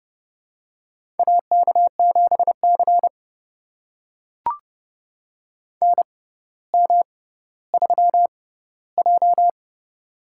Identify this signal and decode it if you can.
Morse code, heard as a keyed tone, about 28 wpm, spaced as for 15 wpm, 700 Hz. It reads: AK7C NM3J
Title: USA Call Signs - 6x Character Spacing - 30wpm